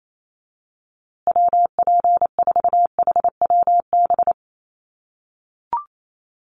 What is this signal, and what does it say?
Morse code, heard as a keyed tone, about 28 wpm, 700 Hz. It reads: WP4HWB
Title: USA Call Signs Repeated 2x 28wpm